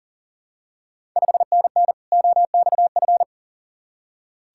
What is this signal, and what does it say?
Morse code, heard as a keyed tone, about 38 wpm, 700 Hz. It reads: 5NN OXF
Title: Ontario QSO Party Exchange 40wpm